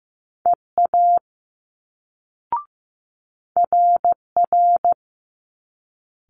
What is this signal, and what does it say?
Morse code, heard as a keyed tone, about 15 wpm, 700 Hz. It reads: EA RR
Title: Multi-letter Phonemes Quick Fire 15wpm